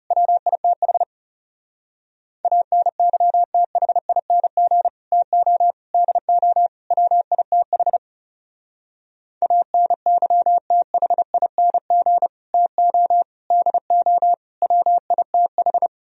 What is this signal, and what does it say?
Morse code, heard as a keyed tone, about 34 wpm, 700 Hz. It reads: WITH ANYTHING TO DO WITH ANYTHING TO DO WITH
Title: Speed-Racing - Sets of 4 Words 30wpm